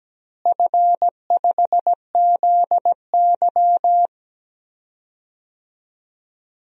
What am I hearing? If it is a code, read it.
Morse code, heard as a keyed tone, about 17 wpm, 700 Hz. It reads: F5ZY
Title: DXCC-100 Call Signs - 4 Characters - Repeated 2x - 17wpm